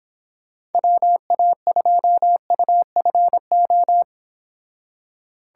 Morse code, heard as a keyed tone, about 31 wpm, 700 Hz. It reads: WA2UFO